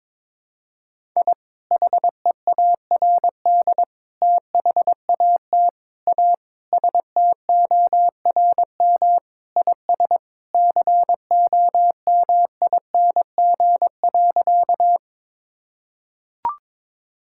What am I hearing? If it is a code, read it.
Morse code, heard as a keyed tone, about 22 wpm, 700 Hz. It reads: I HEARD THAT A STORM IS COMING.
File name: practice / 22wpm / Sentences with 2-Word Phrases - 22wpm